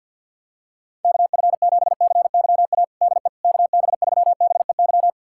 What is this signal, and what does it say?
Morse code, heard as a keyed tone, about 47 wpm, 700 Hz. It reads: KF7CYA DE KB3BEY
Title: USA Callsign DE Callsign - 6 Characters - 50wpm